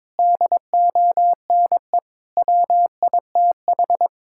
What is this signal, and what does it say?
Morse code, heard as a keyed tone, about 22 wpm, 700 Hz. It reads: DONE WITH
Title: Sentences from Top 1000 Words - Review Part 5 - 22wpm